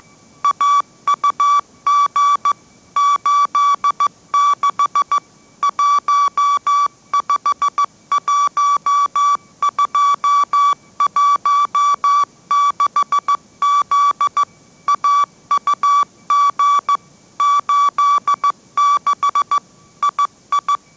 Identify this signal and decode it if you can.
Morse code, heard as a keyed tone, about 15 wpm, 1150 Hz. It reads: AUG 86 151216Z AUG 86 II
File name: mrx / mrx065 16wpm